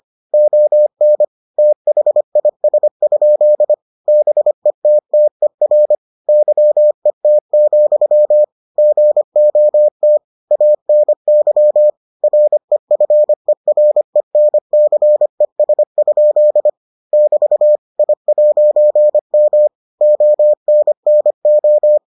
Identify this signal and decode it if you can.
Morse code, heard as a keyed tone, about 25 wpm, 600 Hz. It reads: ON THIS? BETTER YET, GOT ANY REFERENCES? <BT> I'M ONNO